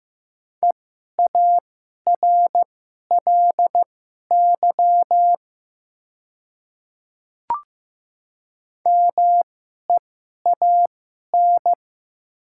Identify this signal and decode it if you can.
Morse code, heard as a keyed tone, about 15 wpm, spaced as for 11 wpm, 700 Hz. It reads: EARLY MEAN
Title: 2x Character Spacing - Top 500 Words - 15wpm